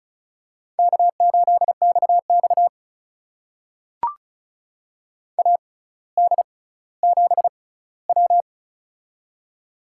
Morse code, heard as a keyed tone, about 36 wpm, spaced as for 18 wpm, 700 Hz. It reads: K8XX AD7W
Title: USA Call Signs - 4 Characters - 6x Character Spacing - 35wpm